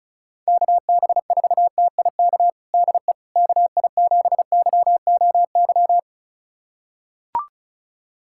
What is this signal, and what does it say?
Morse code, heard as a keyed tone, about 36 wpm, 700 Hz. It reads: KB4TIK DE KI7YOY
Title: USA Callsign DE Callsign - 6 Characters - 35wpm